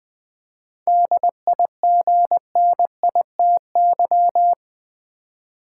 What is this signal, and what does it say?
Morse code, heard as a keyed tone, about 20 wpm, 700 Hz. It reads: DIGNITY
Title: 7 Letter Words - First 3 Letters Given - 20wpm